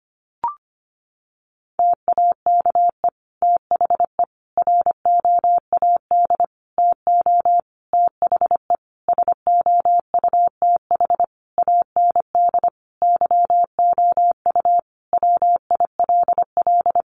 Morse code, heard as a keyed tone, about 25 wpm, 700 Hz. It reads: TAKE THE ROAD TO THE SOUTH AND YOU WILL
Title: Sentences from Top 400 Words 25wpm